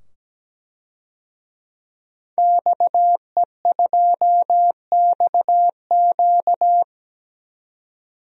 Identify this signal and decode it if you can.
Morse code, heard as a keyed tone, about 17 wpm, 700 Hz. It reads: XE2XQ
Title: DXCC-100 Call Signs - 5 Characters - 17wpm